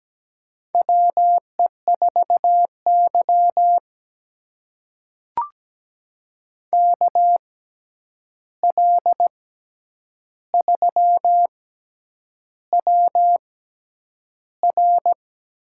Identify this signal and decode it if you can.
Morse code, heard as a keyed tone, about 17 wpm, spaced as for 8 wpm, 700 Hz. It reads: WE4Y KL3WR
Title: USA Call Signs - 6x Character Spacing - 17wpm